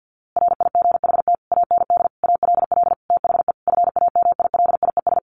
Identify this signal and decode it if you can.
Morse code, heard as a keyed tone, about 50 wpm, 700 Hz. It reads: RIGHT AND ALL THE FAMILIES
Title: A Sound Of Thunder 50wpm